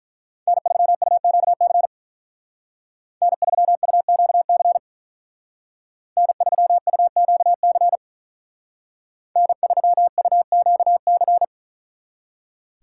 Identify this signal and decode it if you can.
Morse code, heard as a keyed tone, about 44 wpm, 700 Hz. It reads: N3UQC N3UQC N3UQC N3UQC
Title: Speed Racing - USA 1x3 Call Signs 35wpm